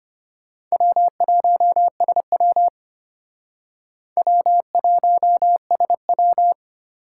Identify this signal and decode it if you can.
Morse code, heard as a keyed tone, about 33 wpm, 700 Hz. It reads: W1SW W1SW